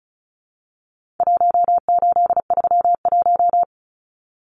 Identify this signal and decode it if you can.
Morse code, heard as a keyed tone, about 34 wpm, 700 Hz. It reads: 1831
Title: Numbers - Contest Style - 4-Digit 35wpm